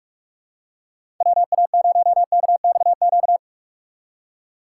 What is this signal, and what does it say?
Morse code, heard as a keyed tone, about 47 wpm, 700 Hz. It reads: WA0KXQ
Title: USA Call Signs - 6 Characters - 2x Character Spacing - 45wpm